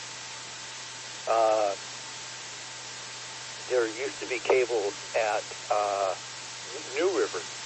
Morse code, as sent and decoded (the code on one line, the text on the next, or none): none